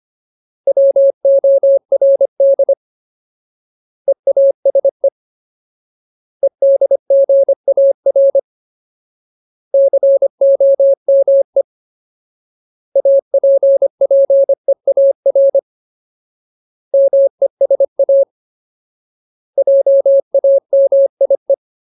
Morse code, heard as a keyed tone, about 25 wpm, 550 Hz. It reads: WORD EASE EDGAR COME APPEAR MESA JAMIE
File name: Ultimate Word List / 25WPM Ultimate